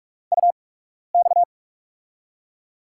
Morse code, heard as a keyed tone, about 44 wpm, spaced as for 12 wpm, 700 Hz. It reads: UX
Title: USA Call Signs - 4 Characters - 8x Character Spacing - 45wpm